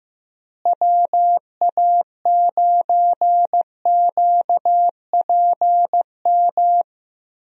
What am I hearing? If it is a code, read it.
Morse code, heard as a keyed tone, about 15 wpm, 700 Hz. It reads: WA9QPM
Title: USA Call Signs - 6 Characters - Repeated 3x - 15wpm